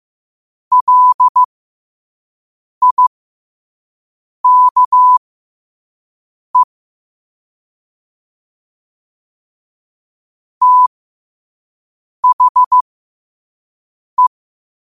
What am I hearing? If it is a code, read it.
Morse code, heard as a keyed tone, about 15 wpm, spaced as for 5 wpm, 1000 Hz. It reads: LIKE THE
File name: qotdfiles / 5wpm